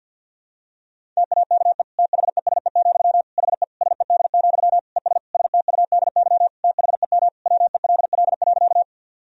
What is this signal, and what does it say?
Morse code, heard as a keyed tone, about 51 wpm, 700 Hz. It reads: TAKE THESE, HE SED, ES STUDY THEM WELL.
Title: Sentences from Top 300 Words and Abbreviations 50wpm